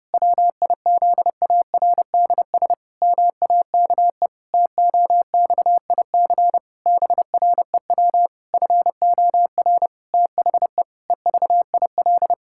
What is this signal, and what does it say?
Morse code, heard as a keyed tone, about 32 wpm, 700 Hz. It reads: WIZARDS MAKE TOXIC BREW FOR THE EVIL
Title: Pangram Challenge v2 - 30wpm